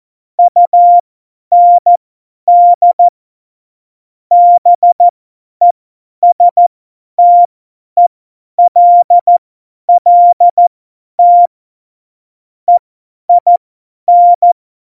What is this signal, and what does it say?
Morse code, heard as a keyed tone, about 14 wpm, spaced as for 10 wpm, 700 Hz. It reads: UND BESTELLT EIN
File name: Morsekurs / Witze / Witz 29. Dezember 2020, 50 F70